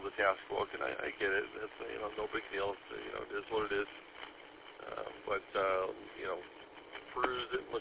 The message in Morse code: none

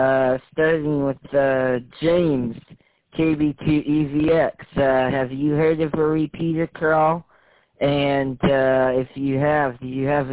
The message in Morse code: none

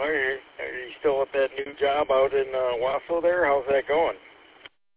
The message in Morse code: none

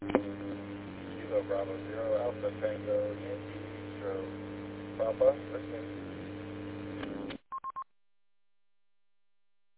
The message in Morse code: none

